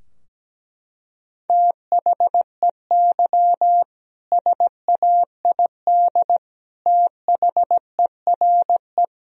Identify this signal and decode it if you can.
Morse code, heard as a keyed tone, about 17 wpm, 700 Hz. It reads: THEY SAID THERE
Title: Sentences from Top 1000 Words - Review Part 2 - 17wpm